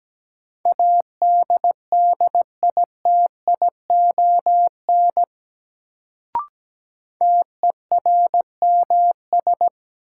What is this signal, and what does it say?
Morse code, heard as a keyed tone, about 17 wpm, 700 Hz. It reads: ADDITION TERMS
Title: Sets of 3 Words - New Word Drill - 17wpm